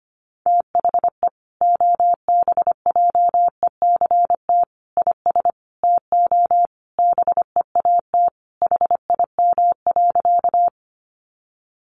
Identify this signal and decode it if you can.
Morse code, heard as a keyed tone, about 25 wpm, 700 Hz. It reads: THE OBJECT IS TO BEAT HIM.
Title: Sentences from Top 1300 Words 25wpm